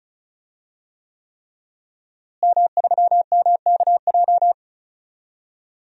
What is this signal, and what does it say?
Morse code, heard as a keyed tone, about 36 wpm, 700 Hz. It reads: M3MKJ
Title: DXCC-100 Call Signs - 5 Characters - Repeated 3x - 35wpm